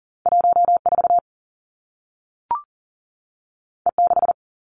Morse code, heard as a keyed tone, about 38 wpm, 700 Hz. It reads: 1 4 E 6
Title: Two Challenging Letter-Number Combinations 40wpm